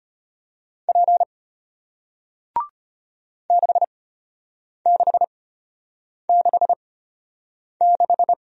Morse code, heard as a keyed tone, about 33 wpm, spaced as for 8 wpm, 700 Hz. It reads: P 6666